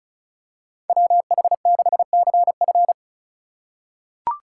Morse code, heard as a keyed tone, about 34 wpm, 690 Hz. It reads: WH6CF
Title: USA Call Signs - 5 Characters - 2x Character Spacing - 35wpm